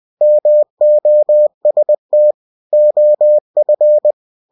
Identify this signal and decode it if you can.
Morse code, heard as a keyed tone, about 20 wpm, 600 Hz. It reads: MOST OF
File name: cw-books / winnie-the-pooh / winnie-20wpm-page-0040